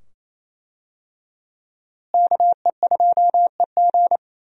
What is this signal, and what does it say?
Morse code, heard as a keyed tone, about 28 wpm, 700 Hz. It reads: KE2EG
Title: USA Call Signs - 5 Characters - Repeated 2x - 28wpm